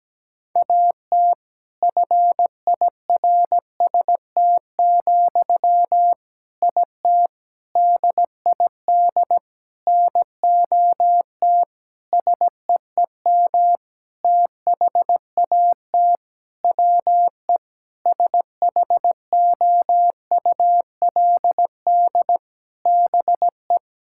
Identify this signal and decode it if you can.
Morse code, heard as a keyed tone, about 17 wpm, 700 Hz. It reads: AT FIRST, IT DID NOT SEEM THAT WE SHOULD BE
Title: Sentences from Top 1000 Words - Review Part 6 - 17wpm